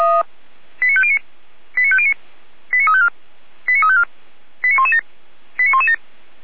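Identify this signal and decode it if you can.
Five-tone selective call (ZVEI), heard as quick runs of five tones, tones of about 70 ms. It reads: 89509 89509 89354 89354 89117 89117
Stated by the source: ZVEI five-tone selective call received by SDR, from the Feuerwehr Orlamünde audio server.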